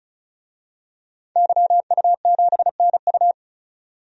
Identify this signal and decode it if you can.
Morse code, heard as a keyed tone, about 36 wpm, 700 Hz. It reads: YU7NU
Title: DXCC-100 Call Signs - 5 Characters - 35wpm